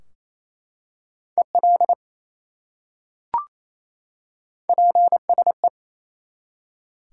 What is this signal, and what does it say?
Morse code, heard as a keyed tone, about 27 wpm, 700 Hz. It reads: EL PSE
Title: Top 180 QSO Elements 28wpm